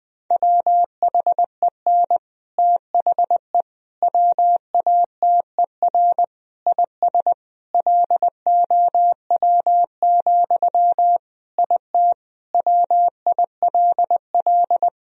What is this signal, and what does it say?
Morse code, heard as a keyed tone, about 21 wpm, 700 Hz. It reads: WHEN THE WATER IS LOW, IT WILL
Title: Sentences from Top 1000 Words - Review Part 1 - 20wpm